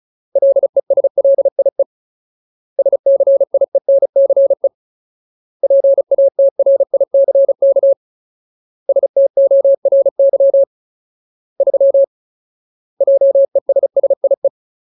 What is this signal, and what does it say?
Morse code, heard as a keyed tone, about 34 wpm, 550 Hz. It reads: LESLIE SCIENCE PATRICK STORY 3 JESSIE
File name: Ultimate Word List / 35WPM Ultimate